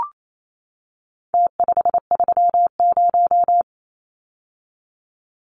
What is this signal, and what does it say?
Morse code, heard as a keyed tone, about 28 wpm, 700 Hz. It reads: T530
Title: Numbers - Contest Style - 4-Digit 28wpm